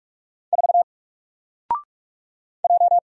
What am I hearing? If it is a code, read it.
Morse code, heard as a keyed tone, about 44 wpm, 700 Hz. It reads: 4 J
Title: Single Letter-Number Alt 45wpm